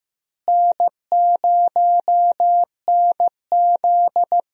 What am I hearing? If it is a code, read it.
Morse code, heard as a keyed tone, about 15 wpm, 700 Hz. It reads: N0NZ
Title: USA Call Signs - 4 Characters - Repeated 2x - 15wpm